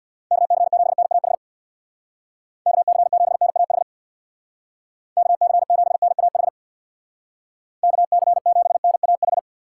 Morse code, heard as a keyed tone, about 63 wpm, 690 Hz. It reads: KC7NAH KC7NAH KC7NAH KC7NAH